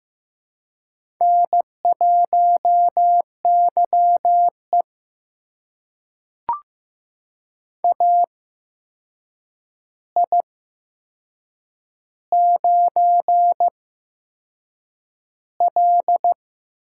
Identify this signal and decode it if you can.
Morse code, heard as a keyed tone, about 15 wpm, spaced as for 6 wpm, 700 Hz. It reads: N1YE AI9L
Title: USA Call Signs - 4 Characters - 8x Character Spacing - 15wpm